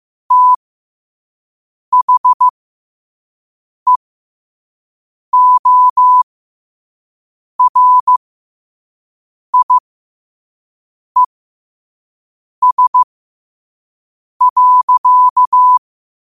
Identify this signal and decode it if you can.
Morse code, heard as a keyed tone, about 15 wpm, spaced as for 5 wpm, 1000 Hz. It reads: THEORIES.
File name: qotdfiles / 5wpm